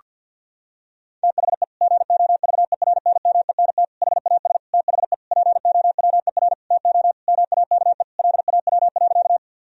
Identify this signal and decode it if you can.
Morse code, heard as a keyed tone, about 46 wpm, 700 Hz. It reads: THE GOVERNMENT HAS THE POWER TO MAKE LAW.
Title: Sentences from Top 1000 Words - Review Part 3 - 50wpm